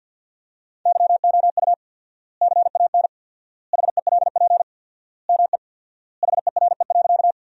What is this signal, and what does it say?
Morse code, heard as a keyed tone, about 52 wpm, 700 Hz. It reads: YOU CAN HELP ME HERE.
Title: 4x Word-Spacing - Sentences from Top 200 Words - 50wpm